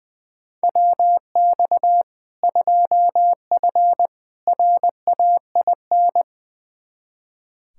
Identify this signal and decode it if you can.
Morse code, heard as a keyed tone, about 20 wpm, 700 Hz. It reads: WX 2F RAIN
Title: K1USN SST QSO Practice 20wpm